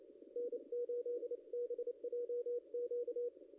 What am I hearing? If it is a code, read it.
Morse code, heard as a keyed tone, about 28 wpm, 500 Hz. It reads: N8BJQ
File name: contests / ss-cw-k5tr-op-n5rz -0600